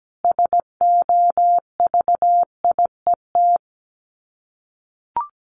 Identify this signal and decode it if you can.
Morse code, heard as a keyed tone, about 17 wpm, 700 Hz. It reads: SOVIET
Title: Sets of 5 Words Encore 17wpm